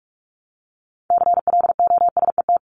Morse code, heard as a keyed tone, about 44 wpm, 700 Hz. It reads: CLOSET
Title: Top 1751-2000 Words 45wpm